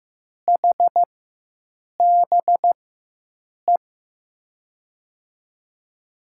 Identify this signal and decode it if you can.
Morse code, heard as a keyed tone, about 15 wpm, spaced as for 7 wpm, 700 Hz. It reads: HBE
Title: USA Call Signs - 4x Character Spacing - 15wpm